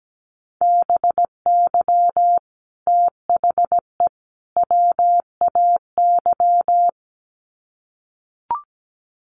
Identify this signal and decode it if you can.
Morse code, heard as a keyed tone, about 17 wpm, 700 Hz. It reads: BY THE WAY
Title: Sets of 3 Words 17wpm